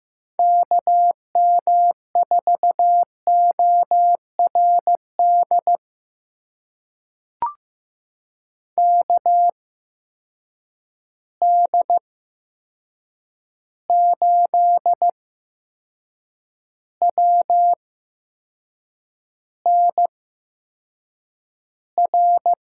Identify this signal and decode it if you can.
Morse code, heard as a keyed tone, about 15 wpm, spaced as for 6 wpm, 700 Hz. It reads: KM4ORD KD8WNR
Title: USA Call Signs - 6 Characters - 8x Character Spacing - 15wpm